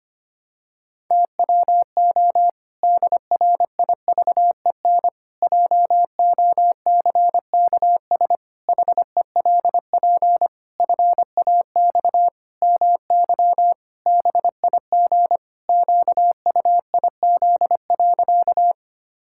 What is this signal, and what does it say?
Morse code, heard as a keyed tone, about 25 wpm, 700 Hz. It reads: TWO DRIVEN JOCKS HELP FAX MY BIG QUIZ.
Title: Pangram Challenge v2 - 25wpm